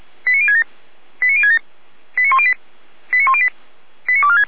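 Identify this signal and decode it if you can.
Five-tone selective call (ZVEI), heard as quick runs of five tones, tones of about 70 ms. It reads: 89067 89067 89108 89108 89246
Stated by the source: ZVEI five-tone selective call received with an SDR.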